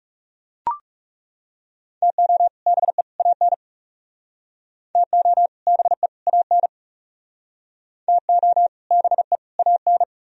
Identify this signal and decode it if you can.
Morse code, heard as a keyed tone, about 36 wpm, 700 Hz. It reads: TO BE AN TO BE AN TO BE AN
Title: Speed-Racing - Sets of 3 Words 30wpm